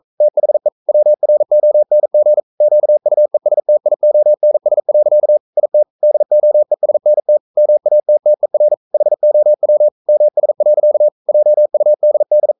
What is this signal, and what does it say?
Morse code, heard as a keyed tone, about 42 wpm, 600 Hz. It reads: THE WRONG QUESTIONS. IT DOESNT MATTER HOW MS. JUDD
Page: Romanen Never af Ken Follet på engelsk.